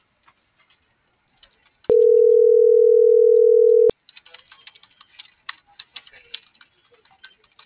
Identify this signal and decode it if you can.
Ringback tone (North America), 440 + 480 Hz.